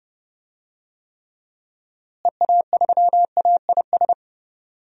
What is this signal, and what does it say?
Morse code, heard as a keyed tone, about 30 wpm, 700 Hz. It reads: EA3AIS
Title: DXCC-100 Call Signs - 6 Characters - Repeated 2x - 30wpm